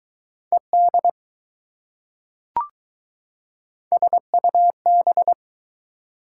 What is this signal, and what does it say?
Morse code, heard as a keyed tone, about 21 wpm, 700 Hz. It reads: ED SUB